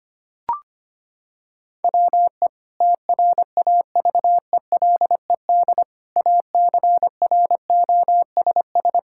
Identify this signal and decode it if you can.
Morse code, heard as a keyed tone, about 25 wpm, 700 Hz. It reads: WE TRAVELED ACROSS
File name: practice / 25wpm / Sentences with 2-Word Phrases - 25wpm